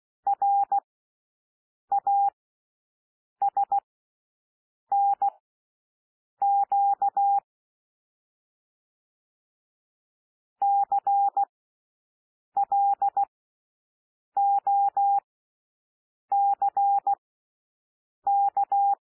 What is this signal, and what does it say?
Morse code, heard as a keyed tone, about 16 wpm, spaced as for 6 wpm, 800 Hz. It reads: RASNQ CLOCK
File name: SOUND LETTERS 16 wpm